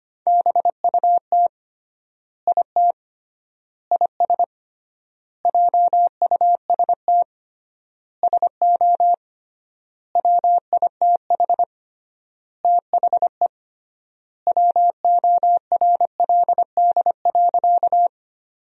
Morse code, heard as a keyed tone, about 25 wpm, 700 Hz. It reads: BUT IT IS JUST SO WITH THE WORLD.
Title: 3x Word-Spacing - Sentences from Top 200 Words - 25wpm